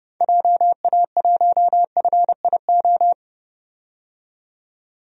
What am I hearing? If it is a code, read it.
Morse code, heard as a keyed tone, about 30 wpm, 700 Hz. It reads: JA1FIO
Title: DXCC-100 Call Signs - 6 Characters - 30wpm